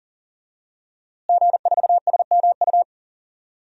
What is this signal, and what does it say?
Morse code, heard as a keyed tone, about 40 wpm, 700 Hz. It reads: G4SMU
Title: DXCC-100 Call Signs - 5 Characters - Repeated 3x - 40wpm